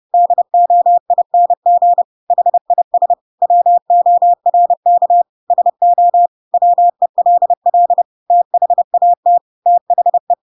Morse code, heard as a keyed tone, about 30 wpm, 700 Hz. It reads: DOING HIS WORK SO WELL THAT THE
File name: Pinocchio / Pinocchio ch8 30 30 ik1hge